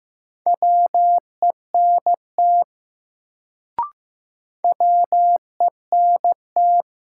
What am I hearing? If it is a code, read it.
Morse code, heard as a keyed tone, about 15 wpm, 700 Hz. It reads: WENT WENT
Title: Sentences from Top 200 Words - New Word Drill - 15wpm